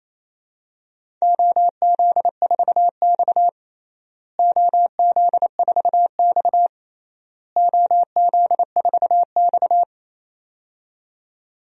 Morse code, heard as a keyed tone, about 27 wpm, 700 Hz. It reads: OZ4X OZ4X OZ4X
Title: DXCC-100 Call Signs - 4 Characters - Repeated 3x - 28wpm